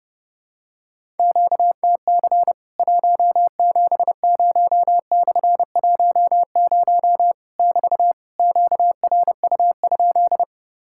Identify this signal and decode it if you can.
Morse code, heard as a keyed tone, about 30 wpm, 700 Hz. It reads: QTC 170/10 <BT> QRU?
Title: Worked All Europe DX Contest - QTC Exchange - 30wpm